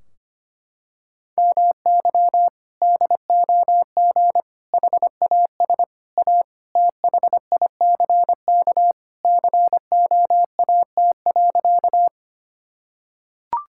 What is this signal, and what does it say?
Morse code, heard as a keyed tone, about 25 wpm, 700 Hz. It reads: MY DOG HAS A THICK COAT.
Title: Sentences from Top 1600 Words 25wpm